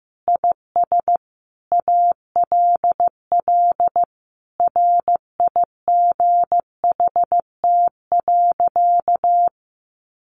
Morse code, heard as a keyed tone, about 15 wpm, 700 Hz. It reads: IS ALL RIGHT.
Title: Sentences from Top 1200 Words 15wpm